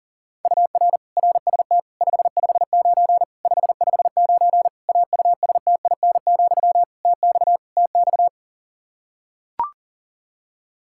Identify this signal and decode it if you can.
Morse code, heard as a keyed tone, about 40 wpm, 700 Hz. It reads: UR RST 559 559 AUSTIN, TX TX
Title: K1USN SST QSO Practice 40wpm